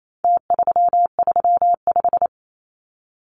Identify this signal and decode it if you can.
Morse code, heard as a keyed tone, about 28 wpm, 700 Hz. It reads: T335
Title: Numbers - Contest Style - 4-Digit 28wpm